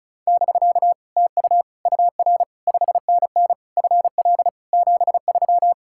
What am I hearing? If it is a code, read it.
Morse code, heard as a keyed tone, about 34 wpm, 700 Hz. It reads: <BK> TU UR 5NN FL 73
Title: Speed-Racing Common QSO Phrases Encore - 35wpm